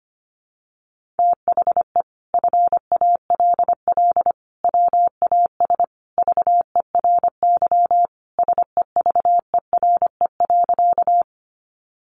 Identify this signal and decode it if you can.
Morse code, heard as a keyed tone, about 25 wpm, 700 Hz. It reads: THE FALL WAS VERY SEVERE.
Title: Sentences from Top 1400 Words 25wpm